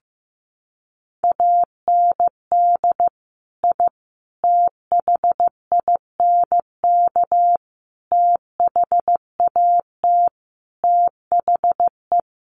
Morse code, heard as a keyed tone, about 15 wpm, 700 Hz. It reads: AND I THINK THAT THE
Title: Sets of 5 Words 15wpm